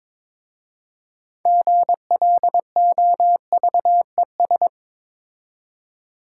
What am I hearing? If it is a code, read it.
Morse code, heard as a keyed tone, about 22 wpm, 700 Hz. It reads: GLOVES
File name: Difficult Words with Hints - 22wpm